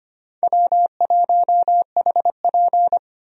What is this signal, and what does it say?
Morse code, heard as a keyed tone, about 25 wpm, 700 Hz. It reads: W1HP
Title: USA Call Signs - 4 Characters - Repeated 3x - 25wpm